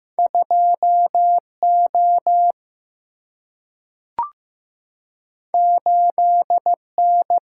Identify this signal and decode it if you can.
Morse code, heard as a keyed tone, about 15 wpm, 700 Hz. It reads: 2O 8N
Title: Single-Letter Single-Number 15wpm